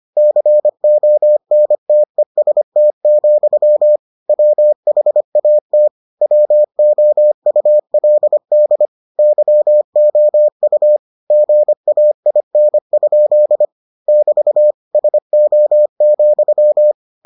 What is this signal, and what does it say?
Morse code, heard as a keyed tone, about 25 wpm, 600 Hz. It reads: CONTEST, WHAT WOULD YOU GAIN? <BT> SO,